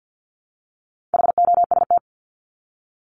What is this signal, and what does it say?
Morse code, heard as a keyed tone, about 50 wpm, 700 Hz. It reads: HOST
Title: Top 1501-1750 Words 50wpm